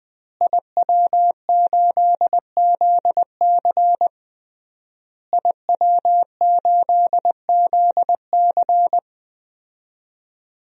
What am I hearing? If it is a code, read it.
Morse code, heard as a keyed tone, about 20 wpm, 700 Hz. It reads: IW8ZC IW8ZC
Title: DXCC-100 Call Signs - 5 Characters - Repeated 3x - 20wpm